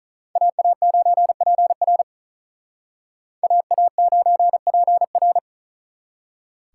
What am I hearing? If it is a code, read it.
Morse code, heard as a keyed tone, about 47 wpm, 700 Hz. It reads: AA9PR AA9PR